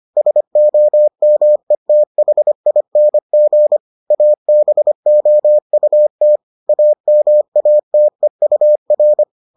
Morse code, heard as a keyed tone, about 25 wpm, 600 Hz. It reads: SOMETHING ABOUT AMATEUR